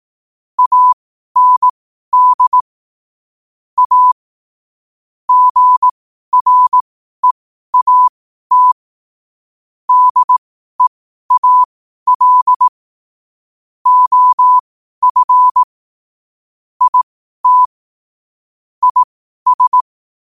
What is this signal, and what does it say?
Morse code, heard as a keyed tone, about 18 wpm, spaced as for 12 wpm, 1000 Hz. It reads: AND A GREAT DEAL OF IT IS
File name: qotdfiles / 12wpm